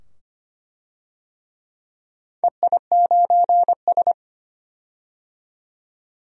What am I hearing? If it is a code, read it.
Morse code, heard as a keyed tone, about 25 wpm, 700 Hz. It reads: EI9S
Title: DXCC-100 Call Signs - 4 Characters - 25wpm